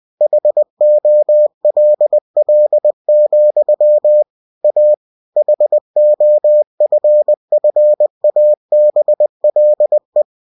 Morse code, heard as a keyed tone, about 20 wpm, 600 Hz. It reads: HOLL, A HOFFABLE